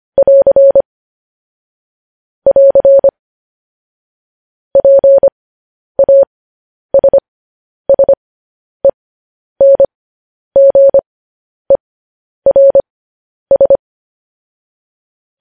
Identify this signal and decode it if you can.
Morse code, heard as a keyed tone, about 25 wpm, spaced as for 10 wpm, 550 Hz. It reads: <AR> <AR> PASSENGERS
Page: News Headlines in Morse Code at 10 WPM